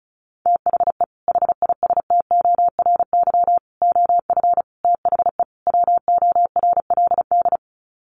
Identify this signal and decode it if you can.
Morse code, heard as a keyed tone, about 34 wpm, 700 Hz. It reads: THE HISTORY OF THE WORLD
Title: Sets of 5 Words Encore2 35wpm